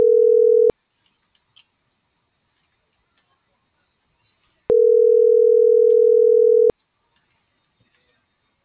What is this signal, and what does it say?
Ringback tone (North America), 440 + 480 Hz.